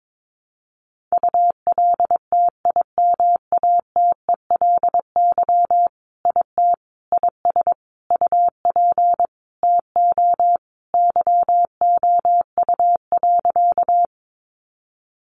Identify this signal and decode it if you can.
Morse code, heard as a keyed tone, about 22 wpm, 700 Hz. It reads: ULTIMATELY IT IS UP TO YOU.
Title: Sentences from Top 1200 Words 22wpm